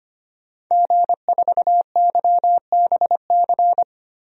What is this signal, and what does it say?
Morse code, heard as a keyed tone, about 25 wpm, 700 Hz. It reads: G4YBC
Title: DXCC-100 Call Signs - 5 Characters - Repeated 2x - 25wpm